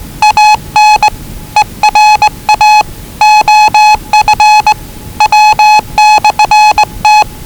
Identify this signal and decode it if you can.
Morse code, heard as a keyed tone, about 20 wpm, 840 Hz. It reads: AN ERA OF W/T
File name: WellingtonRadioZlwCloseDownOnHfAndMf / 01-Zlw 4277khz 417.5 qrt until 0925z